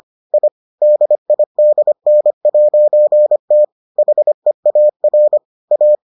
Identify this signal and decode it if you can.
Morse code, heard as a keyed tone, about 25 wpm, 600 Hz. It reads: I DIDN'T HEAR A